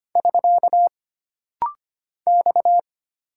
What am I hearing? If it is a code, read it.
Morse code, heard as a keyed tone, about 25 wpm, 700 Hz. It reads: <SK> X
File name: Single Character (Letter:Number:Punctuation:Prosign) - Rapid-Fire - 25wpm